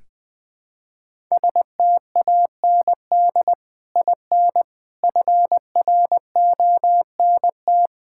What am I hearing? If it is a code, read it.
Morse code, heard as a keyed tone, about 20 wpm, 700 Hz. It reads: STAND IN FRONT
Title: Sentences with 3-Word Phrases - Spoken Beginning - 20wpm